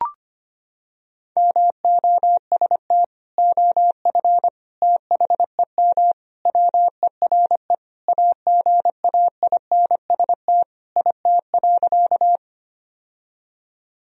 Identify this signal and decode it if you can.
Morse code, heard as a keyed tone, about 25 wpm, 700 Hz. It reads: MOST OF THEM WERE AGAINST IT.
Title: Sentences from Top 1000 Words - Review Part 1 - 25wpm